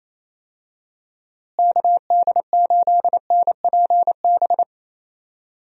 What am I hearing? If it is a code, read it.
Morse code, heard as a keyed tone, about 28 wpm, 700 Hz. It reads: KD8NPB
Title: USA Call Signs - 6 Characters - Repeated 3x - 28wpm